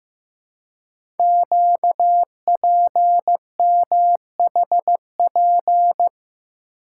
Morse code, heard as a keyed tone, about 15 wpm, 700 Hz. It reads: QPMHP
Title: Code Groups - 5 Letters - 15wpm